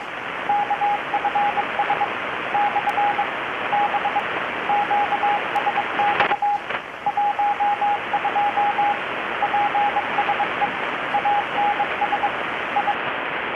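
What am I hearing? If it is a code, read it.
Morse code, heard as a keyed tone, about 22 wpm, 820 Hz, with radio noise. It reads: KFS / B QSX 12 PSE ANS I
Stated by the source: Morse from KPH's Night of Nights XX maritime radiotelegraphy event, received in Chiapas, Mexico about 3,800 km away on a Kenwood R-600 with a grounded 10 m long-wire antenna.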